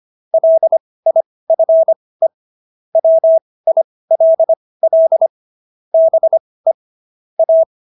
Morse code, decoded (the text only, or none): LIFE WILL BE A